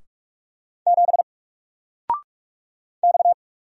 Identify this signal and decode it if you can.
Morse code, heard as a keyed tone, about 46 wpm, 700 Hz. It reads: 7 X
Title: Single Letter-Number 45wpm